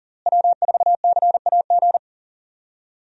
Morse code, heard as a keyed tone, about 37 wpm, 700 Hz. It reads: W4CAG
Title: USA Call Signs - 5 Characters - Repeated 2x - 40wpm